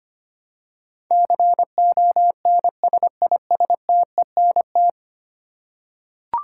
Morse code, heard as a keyed tone, about 25 wpm, 700 Hz. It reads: CONSISTENT
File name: Ten Letter Words - 25wpm